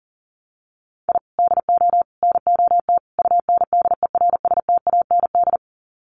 Morse code, heard as a keyed tone, about 40 wpm, 700 Hz. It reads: I DO NOT UNDERSTAND
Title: Sets of 4 Words 40wpm